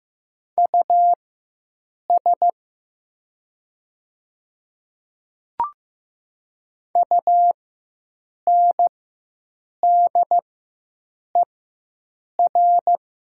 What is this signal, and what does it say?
Morse code, heard as a keyed tone, about 15 wpm, spaced as for 7 wpm, 700 Hz. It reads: US UNDER
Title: 4x Character Spacing - Top 200 Words - 15wpm